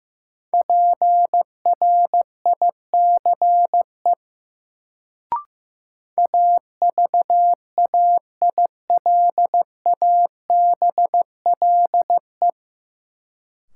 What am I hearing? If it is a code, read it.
Morse code, heard as a keyed tone, about 15 wpm, 700 Hz. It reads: PRICE AVAILABLE
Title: Sentences from Top 600 Words - New Word Drill - 15wpm